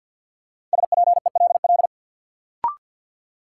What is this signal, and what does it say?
Morse code, heard as a keyed tone, about 50 wpm, 700 Hz. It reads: SPELL
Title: Sentences from Top 1000 Words - Review Part 1 - New Word Drill - 50wpm